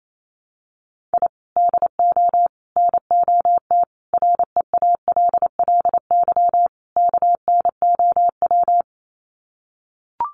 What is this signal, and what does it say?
Morse code, heard as a keyed tone, about 28 wpm, 700 Hz. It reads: I DO NOT REALLY KNOW
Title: Sets of 5 Words Encore 28wpm